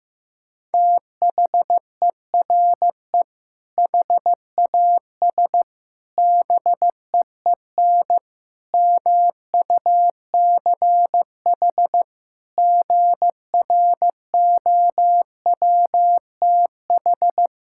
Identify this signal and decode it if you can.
Morse code, heard as a keyed tone, about 15 wpm, 700 Hz. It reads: THERE HAS BEEN MUCH GROWTH
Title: Sentences from Top 1000 Words - Review Part 5 - 15wpm